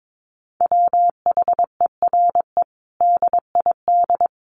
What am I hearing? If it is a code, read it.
Morse code, heard as a keyed tone, about 22 wpm, 700 Hz. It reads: WHERE DID
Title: Sentences from Top 200 Words 22wpm